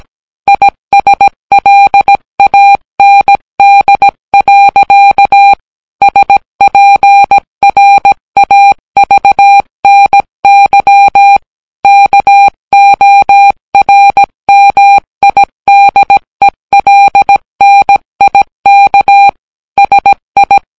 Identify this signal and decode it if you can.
Morse code, heard as a keyed tone, about 17 wpm, 800 Hz. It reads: ISLAND. SPRAVNY KORMIDELNIK SI